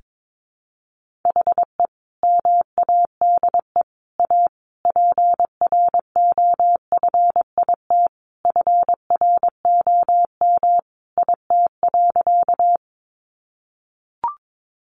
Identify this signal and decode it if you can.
Morse code, heard as a keyed tone, about 22 wpm, 700 Hz. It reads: HE MADE A PROFIT FROM IT.
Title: Sentences from Top 800 Words 22wpm